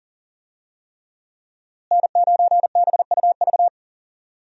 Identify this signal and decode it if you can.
Morse code, heard as a keyed tone, about 40 wpm, 700 Hz. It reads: N9BUV
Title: USA Call Signs - 5 Characters - Repeated 3x - 40wpm